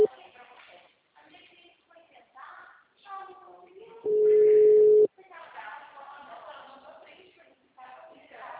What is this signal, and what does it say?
Ringback tone (Europe, 425 Hz), with speech.